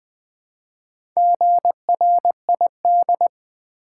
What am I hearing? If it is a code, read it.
Morse code, heard as a keyed tone, about 20 wpm, 700 Hz. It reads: GRID
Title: Build A Word - 2 Letters At A Time - 4-Letter Words - 20wpm